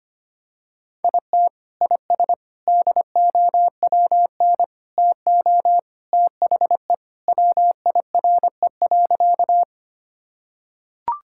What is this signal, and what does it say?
Morse code, heard as a keyed tone, about 25 wpm, 700 Hz. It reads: IT IS DOWN TO THE WIRE.
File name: practice / 25wpm / Sentences with 3-Word Phrases - 25wpm